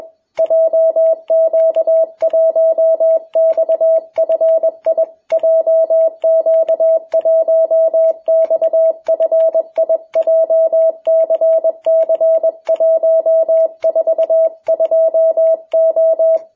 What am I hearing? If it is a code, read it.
Morse code, heard as a keyed tone, about 21 wpm, 660 Hz. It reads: JQ1XFI JQ1XFI JCC142O
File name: jh8cbh / 2021cw